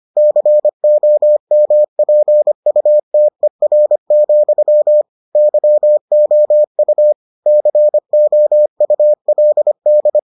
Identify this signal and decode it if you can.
Morse code, heard as a keyed tone, about 25 wpm, 600 Hz. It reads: COMPUTER, YOU COULD